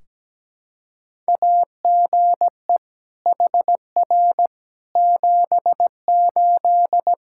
Morse code, heard as a keyed tone, about 17 wpm, 700 Hz. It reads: AGE HR 78
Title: K1USN SST QSO Practice 17wpm